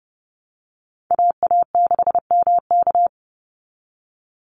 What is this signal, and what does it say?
Morse code, heard as a keyed tone, about 30 wpm, 700 Hz. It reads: AA6MK
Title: Call Signs Repeated 3x - 30wpm